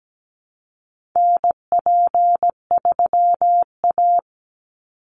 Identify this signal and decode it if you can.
Morse code, heard as a keyed tone, about 17 wpm, 700 Hz. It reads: NP3A
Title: Call Signs Repeated 3x - 17wpm